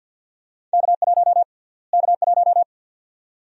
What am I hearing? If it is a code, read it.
Morse code, heard as a keyed tone, about 51 wpm, 700 Hz. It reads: K1 K1
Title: USA Call Sign Prefixes - Repeated 2x - 50wpm